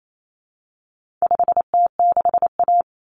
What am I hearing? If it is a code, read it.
Morse code, heard as a keyed tone, about 28 wpm, 700 Hz. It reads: 5T6A
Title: Common Cut Numbers - 4-Digit 28wpm